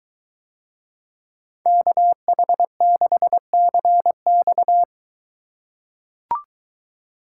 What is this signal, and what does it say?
Morse code, heard as a keyed tone, about 21 wpm, 700 Hz. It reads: KH6CX